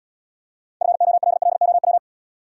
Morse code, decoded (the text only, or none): FOLLOW